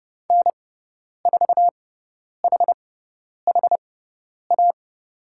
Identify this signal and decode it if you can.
Morse code, heard as a keyed tone, about 30 wpm, spaced as for 10 wpm, 700 Hz. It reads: N4HHA
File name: Lesson 18g - Letter m - Call Signs 30wpm